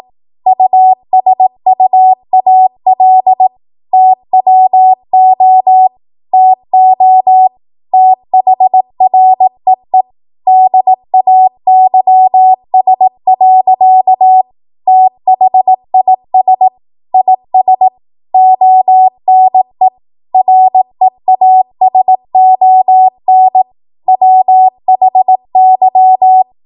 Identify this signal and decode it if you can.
Morse code, heard as a keyed tone, about 18 wpm, 750 Hz. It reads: USUAL TWO TO THREE DAYS. THIS IS ONE REASON WHY
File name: Morse / 18WPM